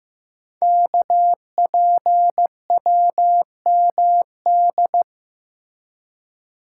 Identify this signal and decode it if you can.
Morse code, heard as a keyed tone, about 15 wpm, 700 Hz. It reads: KPWMD
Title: Code Groups - 5 Letters - 15wpm